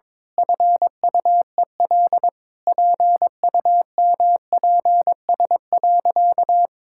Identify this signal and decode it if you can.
Morse code, heard as a keyed tone, about 22 wpm, 700 Hz. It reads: FUEL PUMPS.
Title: Pangram Challenge v2 - 22wpm